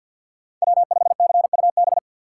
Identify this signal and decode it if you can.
Morse code, heard as a keyed tone, about 53 wpm, 700 Hz. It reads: W5CUB
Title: USA Call Signs - 5 Characters - 4x Character Spacing - 50wpm